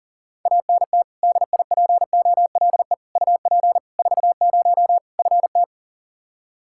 Morse code, heard as a keyed tone, about 40 wpm, 700 Hz. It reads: ANT DIPOLE UP 40 FT
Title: Practice QSO - Exchange 2 - Info - 40wpm